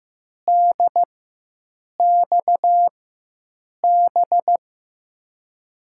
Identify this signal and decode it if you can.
Morse code, heard as a keyed tone, about 15 wpm, spaced as for 7 wpm, 700 Hz. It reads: DXB
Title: USA Call Signs - 4x Character Spacing - 15wpm